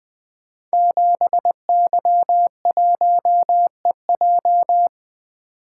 Morse code, heard as a keyed tone, about 20 wpm, 700 Hz. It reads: 7Y1EJ
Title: Code Groups - 5 Letter:Numbers - 20wpm